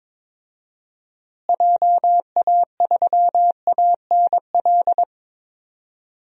Morse code, heard as a keyed tone, about 22 wpm, 700 Hz. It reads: JA3ANL
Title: DXCC-100 Call Signs - 6 Characters - Repeated 2x - 22wpm